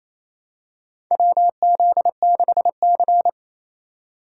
Morse code, heard as a keyed tone, about 28 wpm, 700 Hz. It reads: WZ6C
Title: USA Call Signs - 4 Characters - 2x Character Spacing - 28wpm